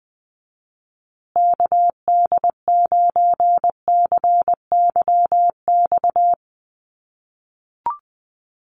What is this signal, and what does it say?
Morse code, heard as a keyed tone, about 20 wpm, 700 Hz. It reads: KD9CYX
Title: USA Call Signs Repeated 3x 20wpm